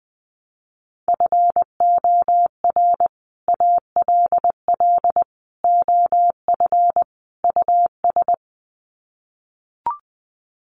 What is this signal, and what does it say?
Morse code, heard as a keyed tone, about 20 wpm, 700 Hz. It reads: FOR ALL OF US
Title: Sets of 4 Words 20wpm